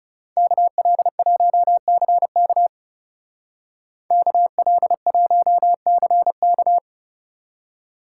Morse code, heard as a keyed tone, about 35 wpm, 700 Hz. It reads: KL1CK KL1CK
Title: Speed Racing - USA 2x2 Call Signs 30wpm